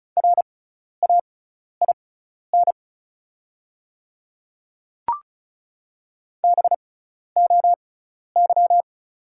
Morse code, heard as a keyed tone, about 36 wpm, spaced as for 12 wpm, 690 Hz. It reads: RAIN BOY
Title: 6x Character Spacing - Top 500 Words - 35wpm